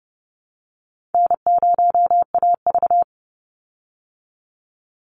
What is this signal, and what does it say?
Morse code, heard as a keyed tone, about 30 wpm, 700 Hz. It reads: N0AV
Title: Easy Call Signs Repeated 3x - 30wpm